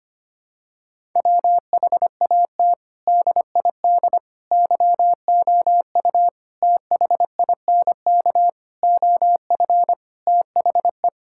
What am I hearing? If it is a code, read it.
Morse code, heard as a keyed tone, about 25 wpm, 700 Hz. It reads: WHAT DID YOU THINK OF THE
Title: Sentences from Top 1000 Words - Review Part 5 - 25wpm